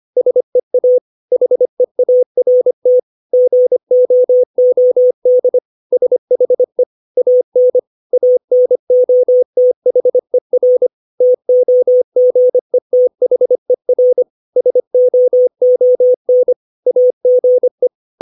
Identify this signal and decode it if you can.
Morse code, heard as a keyed tone, about 25 wpm, 500 Hz. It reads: SEA HEART GOOD SHE AN ANOTHER TOGETHER SOON AGE
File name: 632 words / 632 Words 25WPM